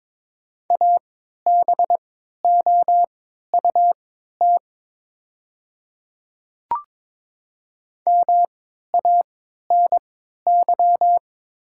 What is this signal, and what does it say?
Morse code, heard as a keyed tone, about 22 wpm, spaced as for 12 wpm, 700 Hz. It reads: ABOUT MANY
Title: 3x Character Spacing - Top 100 Words - 22wpm